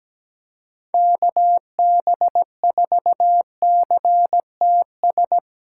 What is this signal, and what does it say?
Morse code, heard as a keyed tone, about 17 wpm, 700 Hz. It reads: KB4CTS
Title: USA Callsign DE Callsign 17wpm